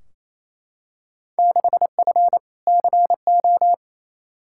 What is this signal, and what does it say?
Morse code, heard as a keyed tone, about 28 wpm, 700 Hz. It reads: 6F CO
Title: Field Day Exchanges Encore - 28wpm